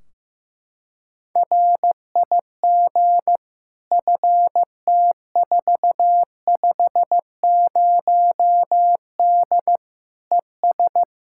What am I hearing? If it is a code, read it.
Morse code, heard as a keyed tone, about 16 wpm, 700 Hz. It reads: RIG FT450D ES
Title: Practice QSO - Exchange 2 - Info - 15wpm